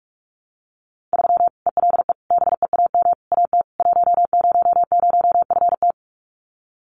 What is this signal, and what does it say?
Morse code, heard as a keyed tone, about 44 wpm, 700 Hz. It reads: 3 ELE BEAM AT 100FT
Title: Common QSO Phrases 45wpm